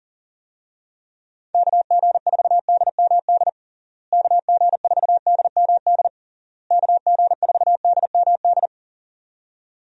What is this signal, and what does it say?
Morse code, heard as a keyed tone, about 40 wpm, 700 Hz. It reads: KG4DMD KG4DMD KG4DMD
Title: USA Call Signs - 6 Characters - Repeated 3x - 40wpm